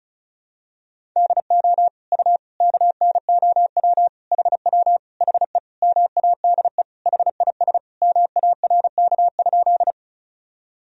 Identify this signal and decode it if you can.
Morse code, heard as a keyed tone, about 36 wpm, 700 Hz. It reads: DO U KNOW HW HE MADE HIS MARK?
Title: Sentences from Top 300 Words and Abbreviations 35wpm